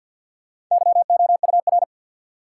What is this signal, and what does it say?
Morse code, heard as a keyed tone, about 50 wpm, 700 Hz. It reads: YOUR
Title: Sets of 2 Words - New Word Drill - 50wpm